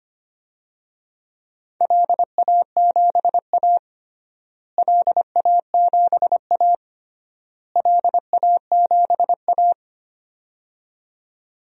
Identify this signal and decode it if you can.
Morse code, heard as a keyed tone, about 25 wpm, 700 Hz. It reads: LA7A LA7A LA7A
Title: DXCC-100 Call Signs - 4 Characters - Repeated 3x - 25wpm